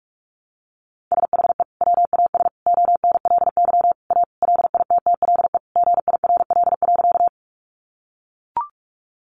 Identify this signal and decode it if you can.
Morse code, heard as a keyed tone, about 44 wpm, 700 Hz. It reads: SHE WAS ONLY A LITTLE GIRL.
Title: Sentences from Top 400 Words 45wpm